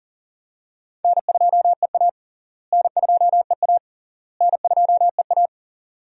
Morse code, heard as a keyed tone, about 40 wpm, 700 Hz. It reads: N2EA N2EA N2EA
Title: USA Call Signs - 4 Characters - Repeated 3x - 40wpm